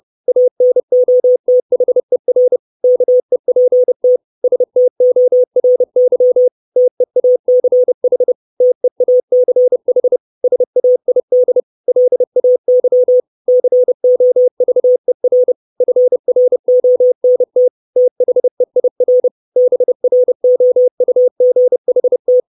Morse code, heard as a keyed tone, about 30 wpm, 500 Hz. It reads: ANOTHER KEPT STORY TEACH TEACH SAID LAY COVER FRONT THEIR BROUGHT
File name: 632 Words 30WPM